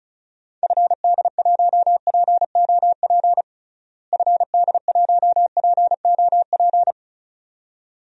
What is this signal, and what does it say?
Morse code, heard as a keyed tone, about 36 wpm, 700 Hz. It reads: FD1POP FD1POP
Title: DXCC-100 Call Signs - 6 Characters - Repeated 2x - 35wpm